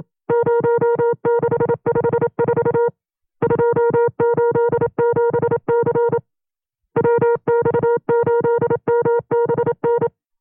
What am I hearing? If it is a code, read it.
Morse code, heard as a keyed tone, about 27 wpm, 470 Hz. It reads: 0654 287C WX8MBN
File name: FreeFiles / CW-practice / 25WPM-side1